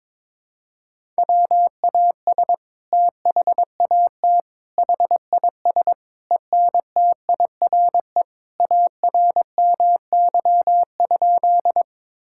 Morse code, heard as a keyed tone, about 22 wpm, 700 Hz. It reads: WAS THAT HIS ENTIRE ARMY?
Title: Sentences from Top 1000 Words - Review Part 5 - 22wpm